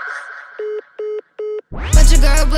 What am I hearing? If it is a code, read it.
There is no code.